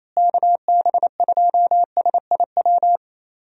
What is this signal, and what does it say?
Morse code, heard as a keyed tone, about 28 wpm, 700 Hz. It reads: KB2SIW
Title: USA Call Signs - 6 Characters - Repeated 3x - 28wpm